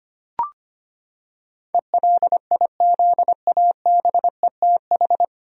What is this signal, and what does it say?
Morse code, heard as a keyed tone, about 25 wpm, 700 Hz. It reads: ELIZABETH
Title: USA Common Names - 25wpm